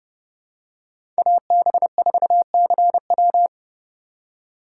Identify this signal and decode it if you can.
Morse code, heard as a keyed tone, about 30 wpm, 700 Hz. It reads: AB4CW
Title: USA Call Signs - 5 Characters - 2x Character Spacing - 30wpm